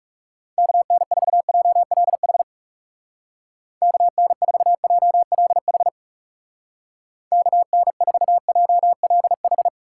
Morse code, heard as a keyed tone, about 36 wpm, 700 Hz. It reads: KN4JLH KN4JLH KN4JLH
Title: Speed Racing - USA 2x3 Call Signs 30wpm